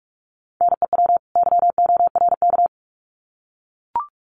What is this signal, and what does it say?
Morse code, heard as a keyed tone, about 44 wpm, 700 Hz. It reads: NEW YORK
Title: Sets of 2 Words 45wpm